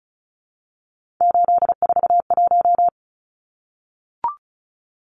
Morse code, heard as a keyed tone, about 34 wpm, 700 Hz. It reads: 841
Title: Numbers - 3-Digit 35wpm